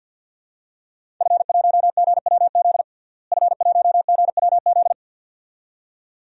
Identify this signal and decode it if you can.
Morse code, heard as a keyed tone, about 53 wpm, 700 Hz. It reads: F1GWZ F1GWZ
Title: DXCC-100 Call Signs - 5 Characters - Repeated 2x - 50wpm